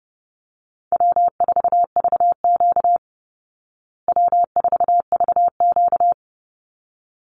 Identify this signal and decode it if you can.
Morse code, heard as a keyed tone, about 30 wpm, 700 Hz. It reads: W4VQ W4VQ
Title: Easy Call Signs Repeated 2x - 30wpm